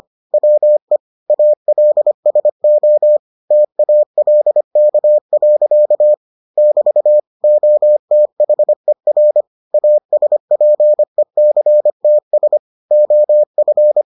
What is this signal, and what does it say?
Morse code, heard as a keyed tone, about 25 wpm, 600 Hz. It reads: WE ALSO TALK. <BT> OTHER ASPECTS OF